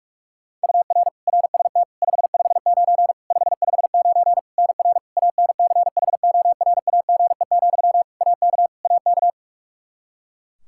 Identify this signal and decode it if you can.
Morse code, heard as a keyed tone, about 46 wpm, 700 Hz. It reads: UR RST 559 559 NR ANCHORAGE, AK AK
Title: Practice QSO - Exchange 1 - Beginning - 45wpm